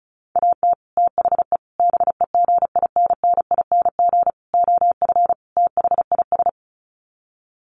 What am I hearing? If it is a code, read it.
Morse code, heard as a keyed tone, about 34 wpm, 700 Hz. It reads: AT THE BEGINNING OF THIS
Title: Sets of 5 Words Encore 35wpm